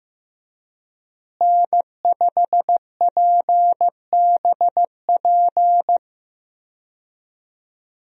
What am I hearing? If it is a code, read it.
Morse code, heard as a keyed tone, about 15 wpm, 700 Hz. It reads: N5PBP
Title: USA Call Signs - 5 Characters - Repeated 3x - 15wpm